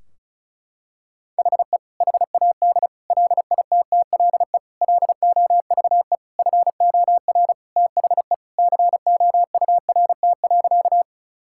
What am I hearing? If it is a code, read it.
Morse code, heard as a keyed tone, about 36 wpm, 700 Hz. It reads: HE HAD LITTLE LOVE FOR THE COURT.
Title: Sentences from Top 1600 Words 35wpm